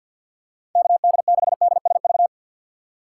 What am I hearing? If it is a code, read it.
Morse code, heard as a keyed tone, about 50 wpm, 700 Hz. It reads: KD6DSV
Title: USA Call Signs - 2x Character Spacing - 50wpm